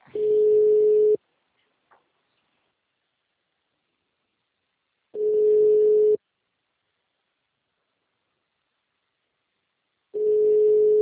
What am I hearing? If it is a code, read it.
Ringback tone (Europe, 425 Hz).